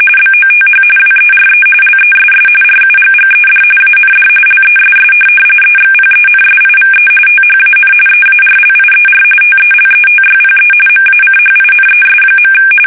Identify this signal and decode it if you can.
RTTY, about 45.45 baud. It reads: BOTH WERE LESS RUGGED AND LESS EXPENSIVE THAN EARLIER TELETYPE MACHINES.
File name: fsk850 / teletype-corp